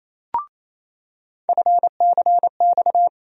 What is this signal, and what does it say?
Morse code, heard as a keyed tone, about 28 wpm, 700 Hz. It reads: FCX
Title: USA Call Sign Postfixes - 3 Letters 28wpm